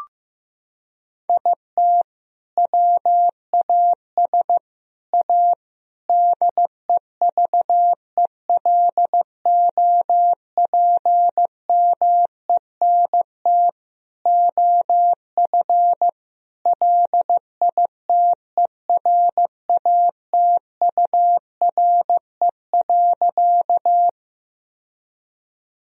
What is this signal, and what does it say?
Morse code, heard as a keyed tone, about 16 wpm, 700 Hz. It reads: IT WAS A DEVELOPMENT OF LITERATURE.
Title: Sentences from Top 1000 Words - Review Part 4 - 15wpm